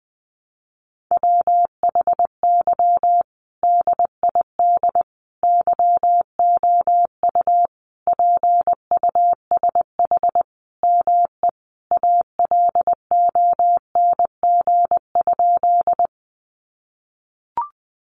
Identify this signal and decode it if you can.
Morse code, heard as a keyed tone, about 20 wpm, 700 Hz. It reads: WHY DID YOU PUSH ME ALONG?
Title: Sentences from Top 1300 Words 20wpm